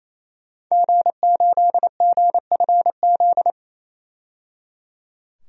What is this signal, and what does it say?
Morse code, heard as a keyed tone, about 28 wpm, 700 Hz. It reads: G8GFZ
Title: DXCC-100 Call Signs - 5 Characters - 28wpm